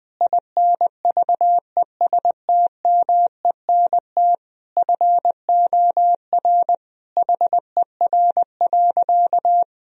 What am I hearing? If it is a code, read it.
Morse code, heard as a keyed tone, about 21 wpm, 700 Hz. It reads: INVESTMENT FOR HER.
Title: Sentences from Top 1000 Words - Review Part 4 - 20wpm